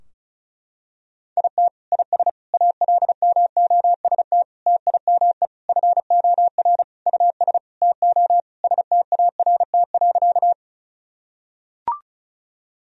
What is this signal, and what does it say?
Morse code, heard as a keyed tone, about 36 wpm, 700 Hz. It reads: IT IS ALMOST TIME FOR US TO START.
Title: Sentences from Top 1000 Words - Review Part 4 - 35wpm